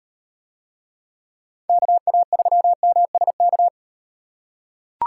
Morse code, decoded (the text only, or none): KA3MSK